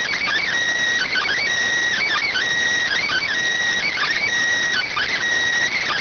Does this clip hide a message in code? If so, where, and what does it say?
Yes: Emergency Alert System (SAME) tones (header bursts) from 1.0 to 5.2 s, heard as harsh buzzing bursts.